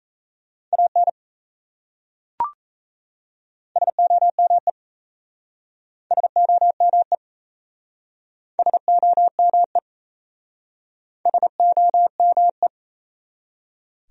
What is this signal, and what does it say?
Morse code, heard as a keyed tone, about 38 wpm, 700 Hz. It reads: AN SOME SOME SOME SOME